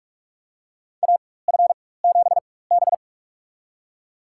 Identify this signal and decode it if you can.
Morse code, heard as a keyed tone, about 46 wpm, spaced as for 21 wpm, 700 Hz. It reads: AF7B
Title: USA Call Signs - 4 Characters - 4x Character Spacing - 45wpm